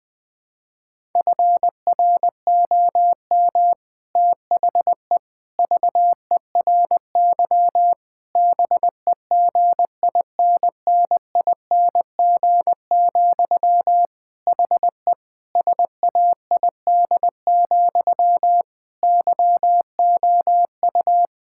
Morse code, heard as a keyed tone, about 21 wpm, 700 Hz. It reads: FROM THE VERY BEGINNING, HE SAID, YOU
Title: Sentences from Top 1600 Words 20wpm